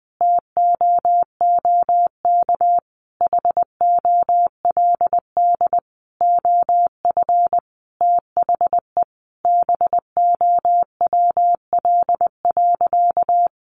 Morse code, heard as a keyed tone, about 20 wpm, 700 Hz. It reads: TOOK HOLD OF THE BOWL.
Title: Sentences from Top 1200 Words 20wpm